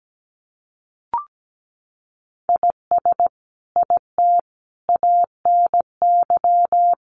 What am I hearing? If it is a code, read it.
Morse code, heard as a keyed tone, about 17 wpm, 700 Hz. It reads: IS IT ANY
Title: Sentences from Top 200 Words 17wpm